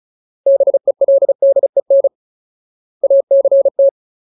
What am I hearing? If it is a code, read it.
Morse code, heard as a keyed tone, about 35 wpm, 550 Hz. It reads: BELDEN ACT
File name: Ultimate Word List / 35WPM Ultimate